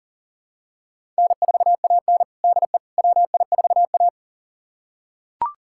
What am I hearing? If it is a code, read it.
Morse code, heard as a keyed tone, about 40 wpm, 700 Hz. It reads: N4AN DE WI4A
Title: USA Callsign DE Callsign - 4 Characters - 40wpm